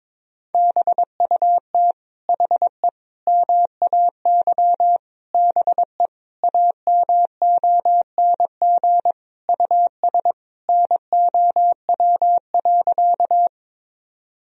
Morse code, heard as a keyed tone, about 22 wpm, 700 Hz. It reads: BUT HE MAY BE AMONG US NOW.
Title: Sentences from Top 1000 Words - Review Part 3 - 22wpm